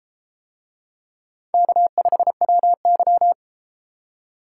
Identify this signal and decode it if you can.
Morse code, heard as a keyed tone, about 32 wpm, 700 Hz. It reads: K5WY